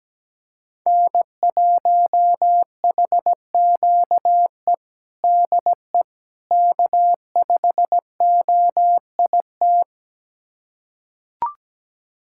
Morse code, heard as a keyed tone, about 17 wpm, 700 Hz. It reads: N1HQE DE K5OIT
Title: USA Callsign DE Callsign - 5 Characters - 17wpm